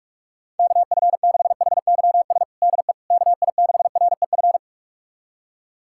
Morse code, heard as a keyed tone, about 46 wpm, 700 Hz. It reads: KF6HYS DE KI6REF
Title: USA Callsign DE Callsign - 6 Characters - 45wpm